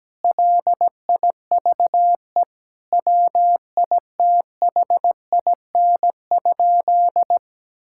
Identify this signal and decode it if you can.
Morse code, heard as a keyed tone, about 17 wpm, 700 Hz. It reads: LIVE WITHIN?
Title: Sentences from Top 1000 Words - Review Part 6 - 17wpm